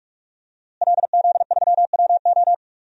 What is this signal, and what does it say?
Morse code, heard as a keyed tone, about 47 wpm, 700 Hz. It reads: LZ3WO
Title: DXCC-100 Call Signs - 5 Characters - Repeated 2x - 45wpm